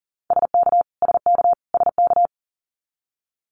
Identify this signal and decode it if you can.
Morse code, heard as a keyed tone, about 40 wpm, 700 Hz. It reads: SK SK SK
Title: Canadian Provinces and Territories - Repeated 3x - 40wpm